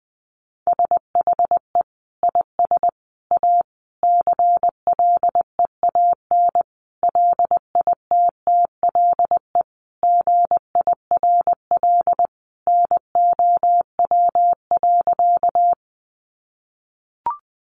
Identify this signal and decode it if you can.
Morse code, heard as a keyed tone, about 20 wpm, 700 Hz. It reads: SHE IS A CLEAN LITTLE GIRL NOW.
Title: Sentences from Top 1100 Words 20wpm